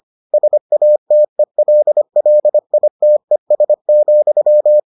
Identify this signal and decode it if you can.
Morse code, heard as a keyed tone, about 25 wpm, 600 Hz. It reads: SATELLITES,